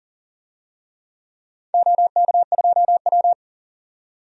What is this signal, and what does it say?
Morse code, heard as a keyed tone, about 40 wpm, 700 Hz. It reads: OK2W
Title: DXCC-100 Call Signs - 4 Characters - 40wpm